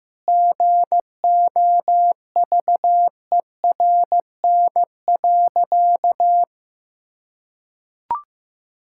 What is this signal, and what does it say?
Morse code, heard as a keyed tone, about 16 wpm, 700 Hz. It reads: GOVERN.
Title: Sentences from Top 1000 Words - Review Part 5 - 15wpm